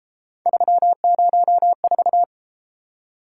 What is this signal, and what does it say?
Morse code, heard as a keyed tone, about 32 wpm, 700 Hz. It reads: 304